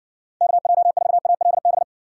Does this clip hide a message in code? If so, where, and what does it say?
Yes: Morse code from 0.4 to 1.8 s, heard as a keyed tone, about 60 wpm, 700 Hz. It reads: KJ4ALB